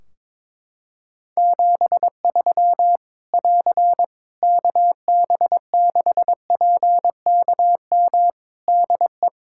Morse code, heard as a keyed tone, about 21 wpm, 700 Hz. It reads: 73 <AR> KB6PKM DE
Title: Practice QSO - Exchange 4 - Ending - 22wpm